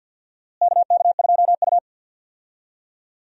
Morse code, heard as a keyed tone, about 46 wpm, 700 Hz. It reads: KK2U
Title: USA Call Signs - 4 Characters - Repeated 2x - 50wpm